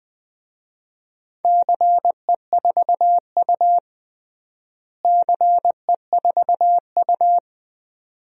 Morse code, heard as a keyed tone, about 20 wpm, 700 Hz. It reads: CE4U CE4U
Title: DXCC-100 Call Signs - 4 Characters - Repeated 2x - 20wpm